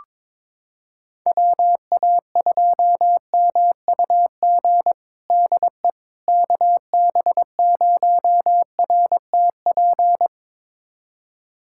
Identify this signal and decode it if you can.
Morse code, heard as a keyed tone, about 21 wpm, 700 Hz. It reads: WA2MUG DE KB0RTP
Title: USA Callsign DE Callsign - 6 Characters - 22wpm